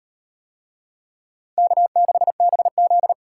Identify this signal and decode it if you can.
Morse code, heard as a keyed tone, about 34 wpm, 700 Hz. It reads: K6BZ